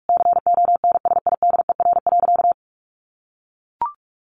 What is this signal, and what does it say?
Morse code, heard as a keyed tone, about 44 wpm, 700 Hz. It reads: CONSIDER.
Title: Sentences from Top 900 Words 45wpm